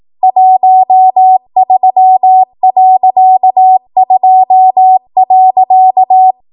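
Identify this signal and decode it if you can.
Morse code, heard as a keyed tone, about 18 wpm, 750 Hz. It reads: 13.2.
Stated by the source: ARRL 18 wpm Morse code practice archive transmission.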